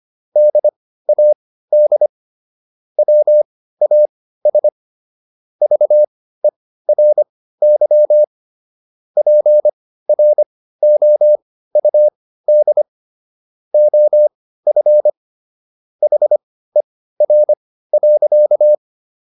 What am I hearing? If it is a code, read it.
Morse code, heard as a keyed tone, about 25 wpm, spaced as for 15 wpm, 600 Hz. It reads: DAD WAS VERY PROUD OF HER.